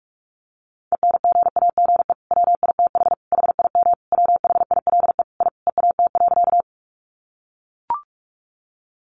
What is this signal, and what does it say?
Morse code, heard as a keyed tone, about 44 wpm, 700 Hz. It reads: ENGAGE WITH HIM WHILE I EAT.
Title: Sentences from Top 1500 Words 45wpm